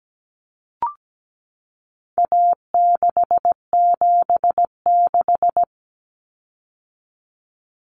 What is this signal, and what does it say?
Morse code, heard as a keyed tone, about 17 wpm, 700 Hz. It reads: A676
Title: Common Cut Numbers - 4-Digit 17wpm